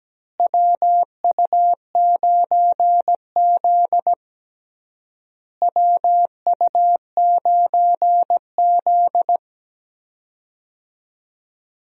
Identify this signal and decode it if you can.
Morse code, heard as a keyed tone, about 17 wpm, 700 Hz. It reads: WU9Z WU9Z
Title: USA Call Signs - 4 Characters - Repeated 3x - 17wpm